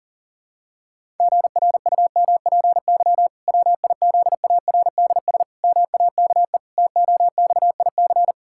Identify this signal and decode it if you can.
Morse code, heard as a keyed tone, about 40 wpm, 700 Hz. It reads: GRUMPY WIZARDS MAKE TOXIC
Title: Pangram Challenge v2 - 40wpm